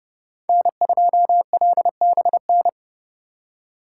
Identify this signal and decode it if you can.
Morse code, heard as a keyed tone, about 35 wpm, 700 Hz. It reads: N2LBN
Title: Speed Racing - USA 1x3 Call Signs 30wpm